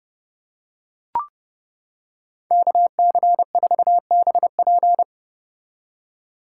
Morse code, heard as a keyed tone, about 31 wpm, 700 Hz. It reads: KC4BP